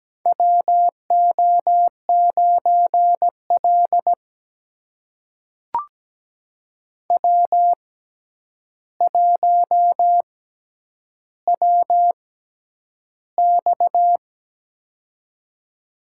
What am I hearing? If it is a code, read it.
Morse code, heard as a keyed tone, about 17 wpm, spaced as for 9 wpm, 700 Hz. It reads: WO9L W1WX
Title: USA Call Signs - 4 Characters - 6x Character Spacing - 17wpm